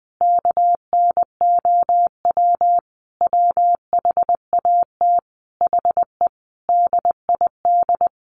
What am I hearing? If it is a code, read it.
Morse code, heard as a keyed tone, about 20 wpm, 700 Hz. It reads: KNOW WHAT HE DID
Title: Sentences from Top 300 Words 20wpm